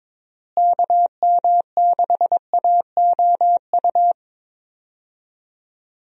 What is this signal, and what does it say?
Morse code, heard as a keyed tone, about 22 wpm, 700 Hz. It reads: KM6AOU
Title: USA Call Signs - 6 Characters - Repeated 3x - 22wpm